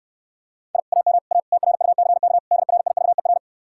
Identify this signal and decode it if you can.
Morse code, heard as a keyed tone, about 69 wpm, 700 Hz. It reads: I AM A TRUCK DRIVER